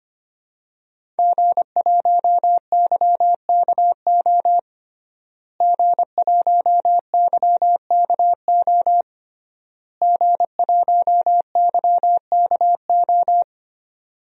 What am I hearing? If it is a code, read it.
Morse code, heard as a keyed tone, about 25 wpm, 700 Hz. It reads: G1YKO G1YKO G1YKO
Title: DXCC-100 Call Signs - 5 Characters - Repeated 3x - 25wpm